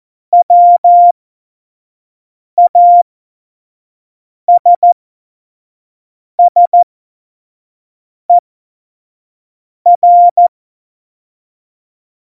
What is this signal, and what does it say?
Morse code, heard as a keyed tone, about 14 wpm, spaced as for 5 wpm, 700 Hz. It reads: WASSER
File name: Morsekurs / Witze / Witz 11. Dezember 2018, 25 F70